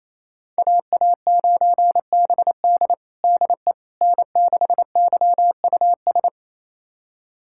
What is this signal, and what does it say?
Morse code, heard as a keyed tone, about 28 wpm, 700 Hz. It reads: AA9BD DE N6YUS
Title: USA Callsign DE Callsign 28wpm